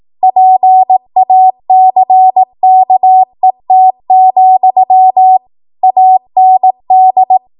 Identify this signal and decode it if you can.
Morse code, heard as a keyed tone, about 18 wpm, 750 Hz. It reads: PACKET, AND
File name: Morse / 18WPM